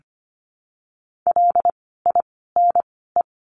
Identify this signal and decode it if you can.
Morse code, heard as a keyed tone, about 25 wpm, spaced as for 16 wpm, 700 Hz. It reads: LINE
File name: course / 25:10wpm / 094 Lesson 16f - Letter c - Words 25wpm